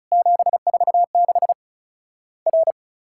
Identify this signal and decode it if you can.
Morse code, heard as a keyed tone, about 36 wpm, 700 Hz. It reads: 746 R
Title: Worked All Europe DX Contest - QTC Exchange - 35wpm